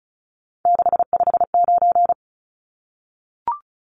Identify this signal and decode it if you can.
Morse code, heard as a keyed tone, about 34 wpm, 700 Hz. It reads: 659
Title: Numbers - 3-Digit 35wpm